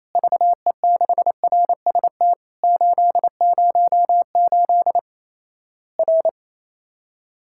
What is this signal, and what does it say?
Morse code, heard as a keyed tone, about 28 wpm, 700 Hz. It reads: VE6RST 808 R
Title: Worked All Europe DX Contest - QTC Exchange - 28wpm